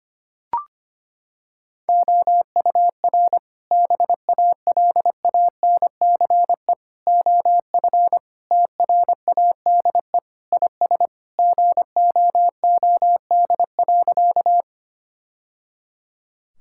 Morse code, heard as a keyed tone, about 25 wpm, 700 Hz. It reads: OUR BALANCE OF TRADE IS GOOD.
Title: Sentences from Top 1000 Words - Review Part 6 - 25wpm